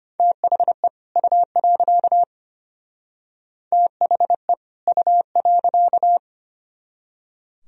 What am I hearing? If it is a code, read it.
Morse code, heard as a keyed tone, about 32 wpm, 700 Hz. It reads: THE U. THE U.